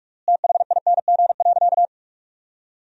Morse code, heard as a keyed tone, about 46 wpm, 700 Hz. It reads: THING.
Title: 3x Word-Spacing - Sentences from Top 200 Words - 45wpm